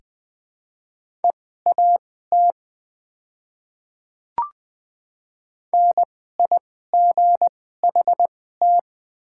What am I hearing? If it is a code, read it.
Morse code, heard as a keyed tone, about 21 wpm, spaced as for 15 wpm, 700 Hz. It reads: EAT NIGHT
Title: 2x Character Spacing - Top 500 Words - 20wpm